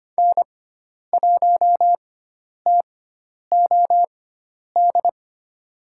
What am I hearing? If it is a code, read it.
Morse code, heard as a keyed tone, about 25 wpm, spaced as for 10 wpm, 700 Hz. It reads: N1TOD
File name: Lesson 16g - Letter c - Call Signs 25wpm